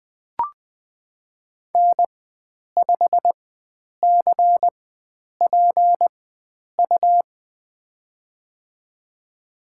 Morse code, heard as a keyed tone, about 20 wpm, spaced as for 9 wpm, 700 Hz. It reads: N5CPU
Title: USA Call Signs - 5 Characters - 4x Character Spacing - 20wpm